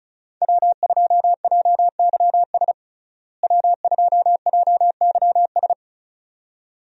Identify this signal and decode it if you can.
Morse code, heard as a keyed tone, about 36 wpm, 700 Hz. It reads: W2JYS W2JYS
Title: USA Call Signs - 5 Characters - Repeated 2x - 35wpm